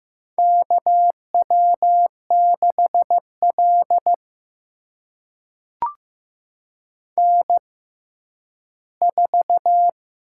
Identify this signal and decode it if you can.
Morse code, heard as a keyed tone, about 15 wpm, spaced as for 10 wpm, 700 Hz. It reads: KW6L N4
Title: USA Call Signs - 4 Characters - 6x Character Spacing - 15wpm